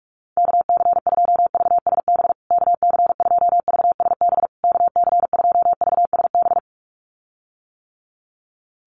Morse code, heard as a keyed tone, about 44 wpm, 700 Hz. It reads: KC2VSB KC2VSB KC2VSB
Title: USA Call Signs Repeated 3x 45wpm